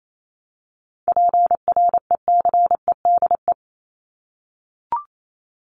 Morse code, heard as a keyed tone, about 28 wpm, 700 Hz. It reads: PRECEDE
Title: ING Suffix - Top 100 Words with Base Words 28wpm